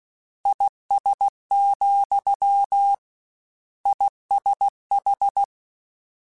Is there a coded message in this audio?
Yes: Morse code from 0.5 to 5.4 s, heard as a keyed tone, about 16 wpm, 800 Hz. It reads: IS, ISH